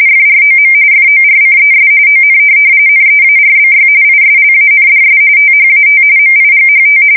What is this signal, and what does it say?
RTTY, about 45.45 baud. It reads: FROM FORCES ON NEARBY ISLANDS THAT MIGHT
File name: fsk170 / USS-Massachusetts